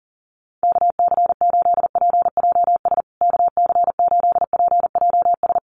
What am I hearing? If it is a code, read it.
Morse code, heard as a keyed tone, about 40 wpm, 700 Hz. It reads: KC8PJS KC8PJS
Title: USA Call Signs Repeated 3x 40wpm